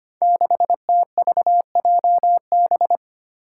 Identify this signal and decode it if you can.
Morse code, heard as a keyed tone, about 25 wpm, 700 Hz. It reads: 6TVJB
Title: Five Challenging Letter-Number Combinations 25wpm